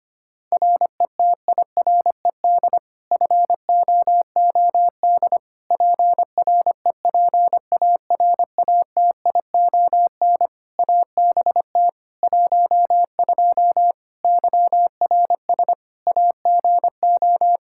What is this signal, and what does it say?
Morse code, heard as a keyed tone, about 25 wpm, 700 Hz. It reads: RETIRED FOOD PREPARATION ABT 12 YRS AGO
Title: K1USN SST QSO Practice 25wpm